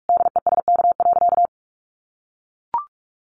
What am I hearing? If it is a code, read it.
Morse code, heard as a keyed tone, about 44 wpm, 700 Hz. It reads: DESK.
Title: Additional Easy Phrases 45wpm